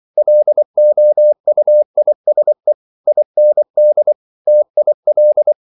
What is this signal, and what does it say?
Morse code, heard as a keyed tone, about 24 wpm, 600 Hz. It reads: LOUISE IND TIL
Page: Ferie 24 wpm